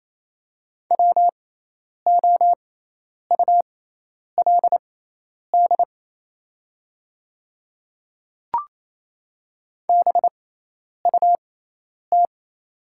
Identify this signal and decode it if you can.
Morse code, heard as a keyed tone, about 28 wpm, spaced as for 10 wpm, 700 Hz. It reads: WOULD BUT
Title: 6x Character Spacing - Top 100 Words - 28wpm